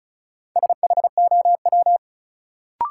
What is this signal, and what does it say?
Morse code, heard as a keyed tone, about 36 wpm, 700 Hz. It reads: SHOW
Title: Sentences from Top 200 Words - New Word Drill - 35wpm